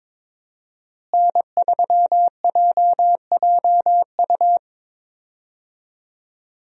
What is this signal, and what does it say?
Morse code, heard as a keyed tone, about 21 wpm, 700 Hz. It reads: N3JJU
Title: USA Call Signs - 5 Characters - Repeated 2x - 22wpm